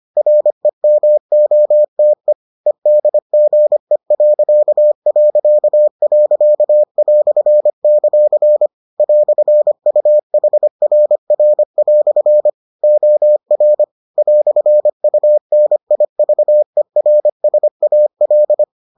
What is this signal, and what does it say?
Morse code, heard as a keyed tone, about 25 wpm, 600 Hz. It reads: REMOTE EDGE..."; "UHRR" OR "UNIVERSAL